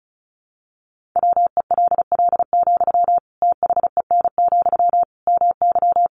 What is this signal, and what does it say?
Morse code, heard as a keyed tone, about 34 wpm, 700 Hz. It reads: WELL, THEN, MY
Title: Sentences from Top 1400 Words 35wpm